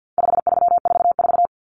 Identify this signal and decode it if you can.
Morse code, heard as a keyed tone, about 48 wpm, 700 Hz. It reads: 5344
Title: Numbers - 4-Digit 50wpm